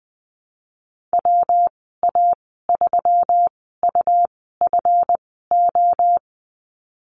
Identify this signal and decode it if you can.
Morse code, heard as a keyed tone, about 20 wpm, spaced as for 15 wpm, 700 Hz. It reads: WA3UFO
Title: 186 Lesson 30g - Letter b - Call Signs 20wpm